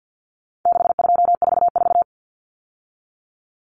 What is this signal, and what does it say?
Morse code, heard as a keyed tone, about 50 wpm, 700 Hz. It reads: 6244
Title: Numbers - 4-Digit 50wpm